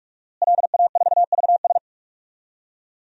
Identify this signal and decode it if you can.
Morse code, heard as a keyed tone, about 46 wpm, 700 Hz. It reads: LA4VS
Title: DXCC-100 Call Signs - 5 Characters - Repeated 2x - 45wpm